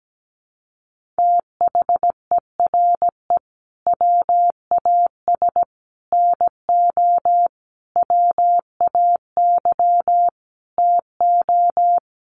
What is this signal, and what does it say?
Morse code, heard as a keyed tone, about 17 wpm, 700 Hz. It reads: THERE WAS NO WAY TO
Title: Sets of 5 Words Encore 17wpm